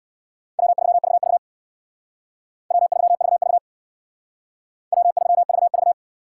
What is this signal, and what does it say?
Morse code, heard as a keyed tone, about 61 wpm, 700 Hz. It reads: W3VV W3VV W3VV
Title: Speed Racing - USA 1x2 Call Signs 50wpm